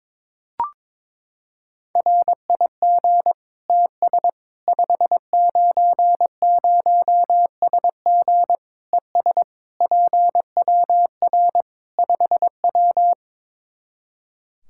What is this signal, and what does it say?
Morse code, heard as a keyed tone, about 22 wpm, 700 Hz. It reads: RIG TS 590SG ES PWR 5W
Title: Practice QSO - Exchange 2 - Info - 22wpm